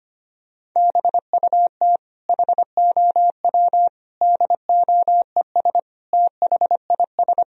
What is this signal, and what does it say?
Morse code, heard as a keyed tone, about 25 wpm, 700 Hz. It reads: BUT HOW DOES THIS
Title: Sentences from Top 1000 Words - Review Part 3 - 25wpm